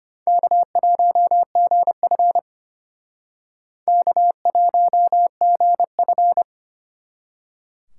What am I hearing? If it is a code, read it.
Morse code, heard as a keyed tone, about 32 wpm, 700 Hz. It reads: K1GF K1GF